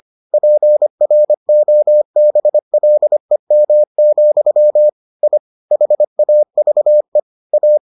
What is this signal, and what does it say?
Morse code, heard as a keyed tone, about 25 wpm, 600 Hz. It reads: PROBLEM, I HAVE A